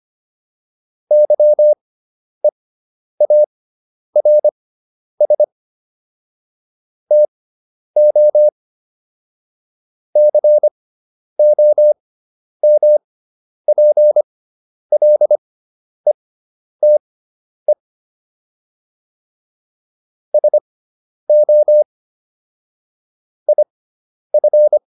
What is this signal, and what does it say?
Morse code, heard as a keyed tone, about 25 wpm, spaced as for 10 wpm, 600 Hz. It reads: YEARS TO COMPLETE SO IF